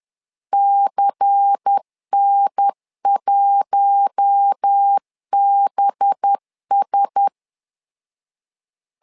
Morse code, heard as a keyed tone, about 10 wpm, 800 Hz. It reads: CN1BS
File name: LEZIONI / 5 LBVC / 5 LBVC 10wpm 2349